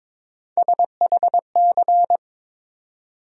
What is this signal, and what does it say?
Morse code, heard as a keyed tone, about 22 wpm, 700 Hz. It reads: SHC
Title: USA Call Sign Postfixes - 3 Letters 22wpm